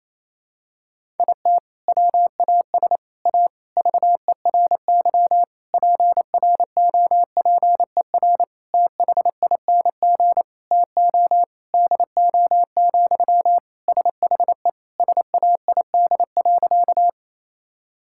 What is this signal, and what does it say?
Morse code, heard as a keyed tone, about 28 wpm, 700 Hz. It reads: IT WAS A VERY PROPER THING TO DO, SHE SAID.
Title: Sentences from Top 1000 Words - Review Part 5 - 28wpm